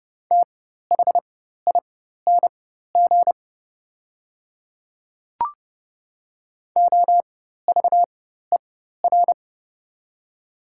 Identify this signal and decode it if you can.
Morse code, heard as a keyed tone, about 32 wpm, spaced as for 14 wpm, 700 Hz. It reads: THING OVER
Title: 4x Character Spacing - Top 100 Words - 30wpm